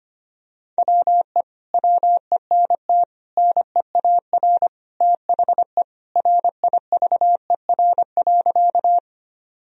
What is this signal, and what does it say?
Morse code, heard as a keyed tone, about 25 wpm, 700 Hz. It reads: WE WENT NEAR THE RIVER.
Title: Sentences from Top 1000 Words - Review Part 1 - 25wpm